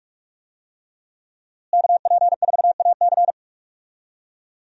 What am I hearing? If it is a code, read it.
Morse code, heard as a keyed tone, about 44 wpm, 700 Hz. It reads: KP4AC
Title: USA Call Signs - 5 Characters - Repeated 3x - 45wpm